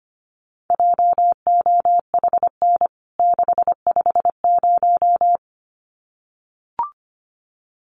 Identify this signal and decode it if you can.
Morse code, heard as a keyed tone, about 25 wpm, 700 Hz. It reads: JOHN 650
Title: CWT Exchange 25wpm